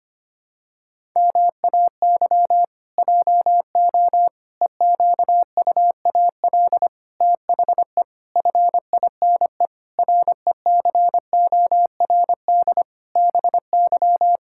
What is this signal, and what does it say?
Morse code, heard as a keyed tone, about 25 wpm, 700 Hz. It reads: MAY JO EQUAL THE FINE RECORD BY
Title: Pangram Challenge v2 - 25wpm